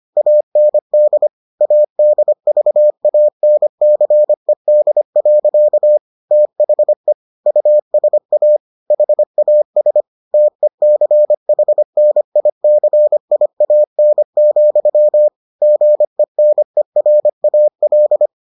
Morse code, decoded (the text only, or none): AND ADVANCED. THE USA HAS TECHNICIAN, GENERAL